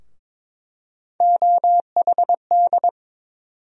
Morse code, heard as a keyed tone, about 22 wpm, 700 Hz. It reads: OHD
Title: USA Call Sign Postfixes - 3 Letters 22wpm